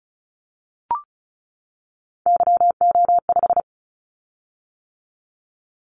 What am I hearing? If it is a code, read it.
Morse code, heard as a keyed tone, about 36 wpm, 700 Hz. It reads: YO5
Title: Callsign Prefixes for Top 25 DXCC - 35wpm